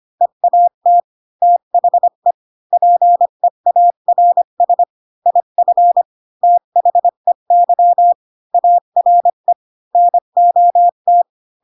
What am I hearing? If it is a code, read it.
Morse code, heard as a keyed tone, about 25 wpm, 700 Hz. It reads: EAT THE PEARS IF THEY ARE NOT
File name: Pinocchio / Pinocchio ch7 20 25 ik1hge